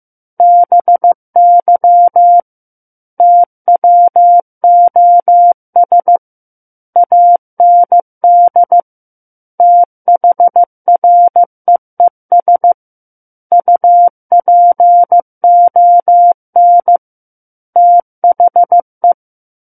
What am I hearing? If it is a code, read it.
Morse code, heard as a keyed tone, about 15 wpm, 700 Hz. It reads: BY TWOS AND THREES UPON THE
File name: Book One / 06-Chapter 6 - 15 WPM